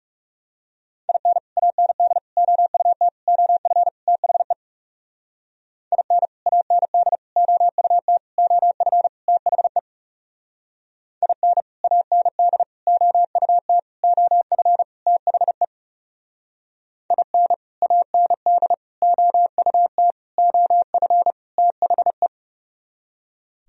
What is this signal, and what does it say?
Morse code, heard as a keyed tone, about 39 wpm, 700 Hz. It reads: IN AND OUT OF THE IN AND OUT OF THE IN AND OUT OF THE IN AND OUT OF THE